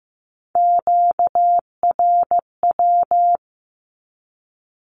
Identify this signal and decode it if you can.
Morse code, heard as a keyed tone, about 15 wpm, 700 Hz. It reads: QRW
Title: Q-Codes with Spoken Meaning - 15wpm